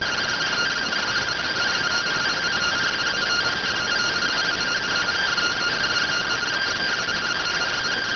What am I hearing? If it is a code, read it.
Two-tone teleprinter signal, about 170 Hz shift.